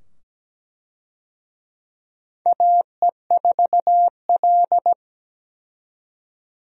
Morse code, heard as a keyed tone, about 17 wpm, 700 Hz. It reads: AE4L
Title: USA Call Signs - 4 Characters - 17wpm